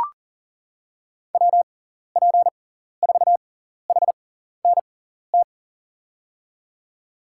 Morse code, heard as a keyed tone, about 37 wpm, spaced as for 14 wpm, 700 Hz. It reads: WP4HNT
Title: USA Call Signs - 6 Characters - 6x Character Spacing - 40wpm